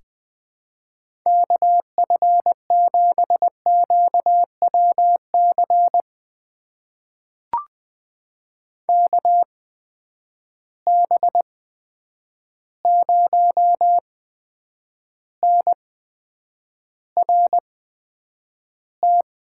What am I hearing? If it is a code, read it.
Morse code, heard as a keyed tone, about 20 wpm, spaced as for 9 wpm, 700 Hz. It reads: KF7QWC KB0NRT
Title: USA Call Signs - 6 Characters - 8x Character Spacing - 20wpm